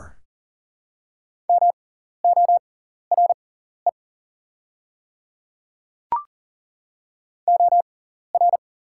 Morse code, heard as a keyed tone, about 40 wpm, spaced as for 14 wpm, 690 Hz. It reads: MORE OR
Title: 6x Character Spacing - Top 100 Words - 40wpm